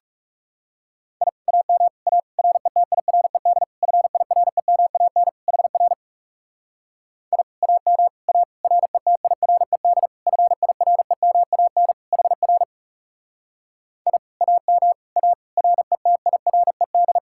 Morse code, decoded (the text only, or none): I AM A RETIRED FIREMAN HR I AM A RETIRED FIREMAN HR I AM A RETIRED